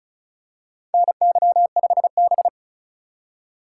Morse code, heard as a keyed tone, about 36 wpm, 700 Hz. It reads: NY5B
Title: USA Call Signs - 4 Characters - 2x Character Spacing - 35wpm